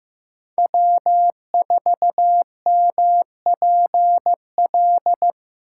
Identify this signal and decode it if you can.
Morse code, heard as a keyed tone, about 18 wpm, 700 Hz. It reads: W4MPL
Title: Speed Racing - USA 1x3 Call Signs 15wpm